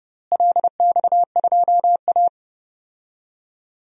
Morse code, heard as a keyed tone, about 30 wpm, 700 Hz. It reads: LX2A
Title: DXCC-100 Call Signs - 4 Characters - Repeated 2x - 30wpm